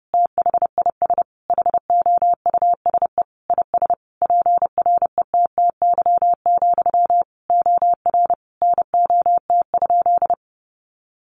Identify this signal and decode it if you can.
Morse code, heard as a keyed tone, about 30 wpm, 700 Hz. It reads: THIS HOUSE IS PRETTY, OR NOT?
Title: Sentences from Top 1000 Words 30wpm